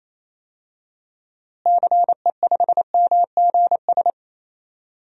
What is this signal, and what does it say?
Morse code, heard as a keyed tone, about 28 wpm, 700 Hz. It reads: CE5MGS
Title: DXCC-100 Call Signs - 6 Characters - Repeated 3x - 28wpm